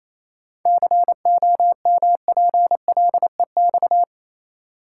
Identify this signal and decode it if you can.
Morse code, heard as a keyed tone, about 28 wpm, 700 Hz. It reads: COMPLEX
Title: Seven Letter Words - 28wpm